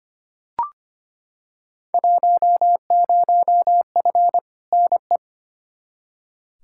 Morse code, heard as a keyed tone, about 25 wpm, 700 Hz. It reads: 10F NE
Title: Field Day Exchanges Encore - 25wpm